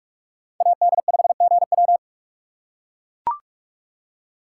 Morse code, heard as a keyed tone, about 43 wpm, 700 Hz. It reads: AD5GW